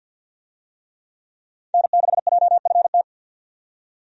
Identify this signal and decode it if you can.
Morse code, heard as a keyed tone, about 49 wpm, 700 Hz. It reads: N6JFT
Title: USA Call Signs - 5 Characters - 50wpm